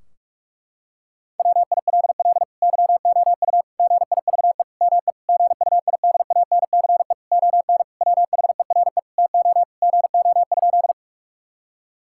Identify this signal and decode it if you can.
Morse code, heard as a keyed tone, about 46 wpm, 700 Hz. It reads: WILL YOU GIVE ME GUIDANCE ON WHERE TO GO?
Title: Sentences from Top 1600 Words 45wpm